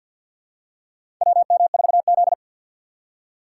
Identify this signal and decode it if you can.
Morse code, heard as a keyed tone, about 53 wpm, 700 Hz. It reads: WM4Z
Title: USA Call Signs - 6x Character Spacing - 50wpm